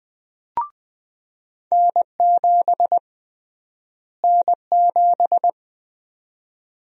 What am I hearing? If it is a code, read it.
Morse code, heard as a keyed tone, about 20 wpm, 700 Hz. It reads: N7 N7
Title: USA Call Sign Prefixes - Repeated 2x - 20wpm